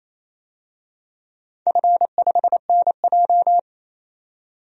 Morse code, heard as a keyed tone, about 28 wpm, 700 Hz. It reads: F5NJ
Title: DXCC-100 Call Signs - 4 Characters - 28wpm